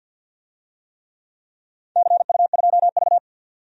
Morse code, heard as a keyed tone, about 51 wpm, 700 Hz. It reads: CU2V
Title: DXCC-100 Call Signs - 4 Characters - 50wpm